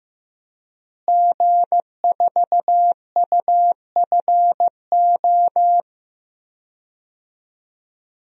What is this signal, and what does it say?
Morse code, heard as a keyed tone, about 16 wpm, 700 Hz. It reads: G4UFO
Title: DXCC-100 Call Signs - 5 Characters - Repeated 2x - 15wpm